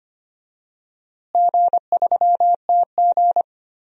Morse code, heard as a keyed tone, about 25 wpm, 700 Hz. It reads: G3TG
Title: DXCC-100 Call Signs - 4 Characters - Repeated 2x - 25wpm